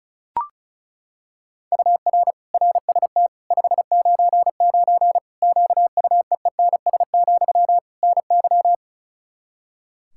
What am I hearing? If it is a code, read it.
Morse code, heard as a keyed tone, about 36 wpm, 700 Hz. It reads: UR RST 599 QUEENS, NY
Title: Practice QSO - Exchange 1 - Beginning - 35wpm